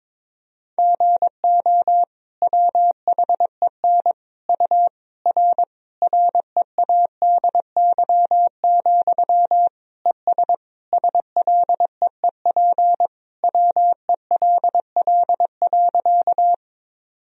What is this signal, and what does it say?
Morse code, heard as a keyed tone, about 22 wpm, 700 Hz. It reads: GO WHEN U R READY, ES SLEEP WELL.
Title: Sentences from Top 500 Words and Abbreviations 22wpm